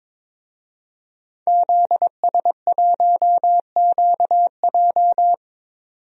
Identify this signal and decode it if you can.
Morse code, heard as a keyed tone, about 22 wpm, 700 Hz. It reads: ZS1QJ
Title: DXCC-100 Call Signs - 5 Characters - 22wpm